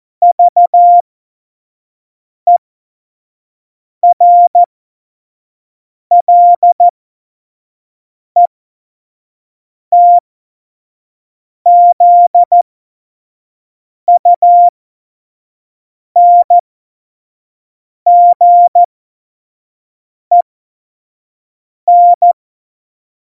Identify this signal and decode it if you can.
Morse code, heard as a keyed tone, about 14 wpm, spaced as for 5 wpm, 700 Hz. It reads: VERLETZUNGEN
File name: Morsekurs / Witze / Witz 01. September 2020, 25 F70